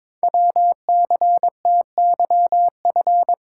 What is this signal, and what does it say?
Morse code, heard as a keyed tone, about 22 wpm, 700 Hz. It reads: WCTYF
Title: Code Groups - 5 Letter:Numbers - 22wpm